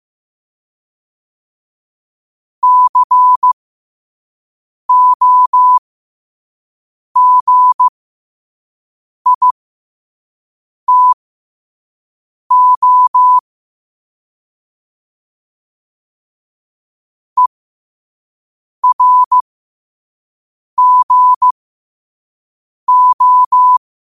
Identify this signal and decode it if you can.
Morse code, heard as a keyed tone, about 16 wpm, spaced as for 5 wpm, 1000 Hz. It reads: COGITO ERGO
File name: qotdfiles / 5wpm